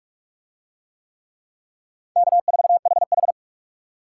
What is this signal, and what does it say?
Morse code, heard as a keyed tone, about 44 wpm, 700 Hz. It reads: K4HH